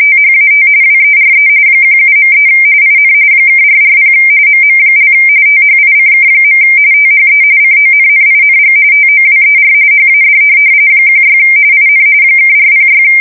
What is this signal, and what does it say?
RTTY, about 45.45 baud. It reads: THAT WERE 17.3 IN (440 MM) THICK. THE CONNING TOWER HAD 16 IN (406